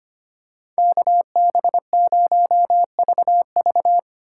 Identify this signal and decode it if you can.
Morse code, heard as a keyed tone, about 25 wpm, 700 Hz. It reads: KB0VV
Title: USA Call Signs - 4x Character Spacing - 25wpm